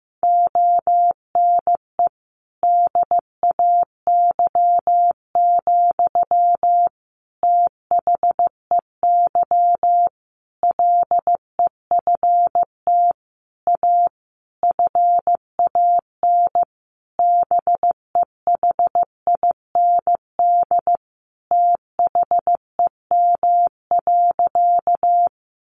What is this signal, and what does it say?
Morse code, heard as a keyed tone, about 15 wpm, 700 Hz. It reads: ONE DAY, THEY LEFT A FAN BEHIND THEM.
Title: Sentences from Top 1300 Words 15wpm